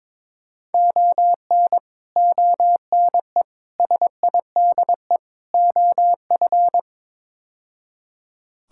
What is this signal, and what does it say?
Morse code, heard as a keyed tone, about 21 wpm, 700 Hz. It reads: ON ONE SIDE OF
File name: practice / 22wpm / Build A Phrase - 5-Word Phrases - 22wpm